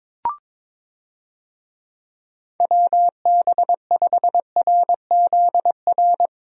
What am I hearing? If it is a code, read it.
Morse code, heard as a keyed tone, about 22 wpm, 700 Hz. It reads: WB5RZR
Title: USA Call Signs - 6 Characters - Repeated 3x - 22wpm